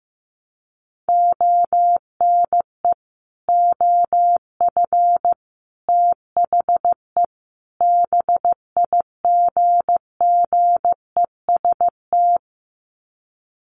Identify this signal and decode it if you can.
Morse code, heard as a keyed tone, about 15 wpm, 700 Hz. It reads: ONE OF THE BIGGEST
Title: Sets of 4 Words 15wpm